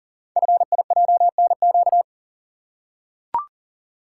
Morse code, heard as a keyed tone, about 40 wpm, 690 Hz. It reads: FIJNQ
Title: Code Groups - 5 Letters - 40wpm